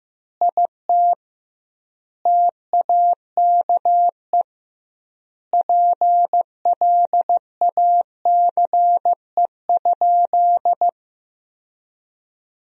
Morse code, heard as a keyed tone, about 15 wpm, 700 Hz. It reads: IT TAKE PLACE?
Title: 2x Word-Spacing - Sentences from Top 200 Words - 15wpm